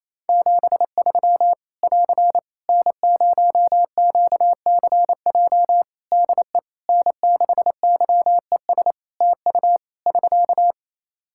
Morse code, heard as a keyed tone, about 28 wpm, 700 Hz. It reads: 73 <AR> N0QCJ DE N6YES TU <SK>
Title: K1USN SST QSO Practice 28wpm